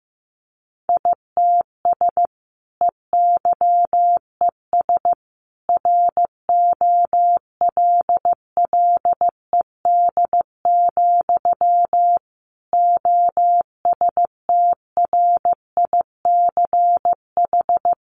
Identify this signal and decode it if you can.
Morse code, heard as a keyed tone, about 15 wpm, 700 Hz. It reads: ITS EYES ROLLED, OSTRICH
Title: A Sound Of Thunder 15wpm